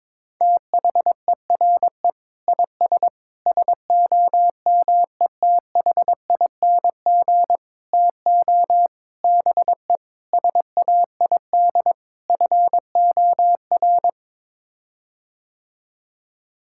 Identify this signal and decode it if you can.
Morse code, heard as a keyed tone, about 22 wpm, 700 Hz. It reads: THERE IS SOMETHING TO BE SAID FOR
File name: practice / 22wpm / Sets of 7 Words - 22wpm